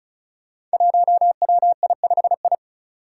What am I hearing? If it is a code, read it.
Morse code, heard as a keyed tone, about 36 wpm, 700 Hz. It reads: 1WI5I
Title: Five Challenging Letter-Number Combinations 35wpm